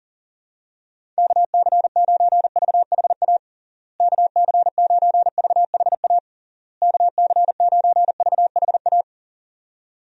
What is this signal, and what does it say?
Morse code, heard as a keyed tone, about 40 wpm, 700 Hz. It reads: KC9VHA KC9VHA KC9VHA
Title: USA Call Signs - 6 Characters - Repeated 3x - 40wpm